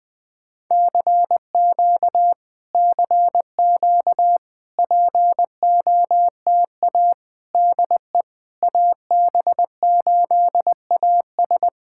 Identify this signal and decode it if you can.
Morse code, heard as a keyed tone, about 20 wpm, 700 Hz. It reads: CQ CQ POTA DE AB8AS
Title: POTA QSO Practice - 20wpm